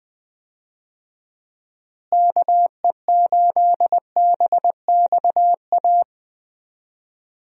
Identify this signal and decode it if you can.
Morse code, heard as a keyed tone, about 20 wpm, 700 Hz. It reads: KE8BXA
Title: USA Call Signs - 6 Characters - Repeated 3x - 20wpm